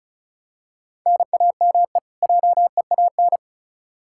Morse code, heard as a keyed tone, about 34 wpm, 700 Hz. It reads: NAME JEAN
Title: Practice QSO - Exchange 1 - Beginning - 35wpm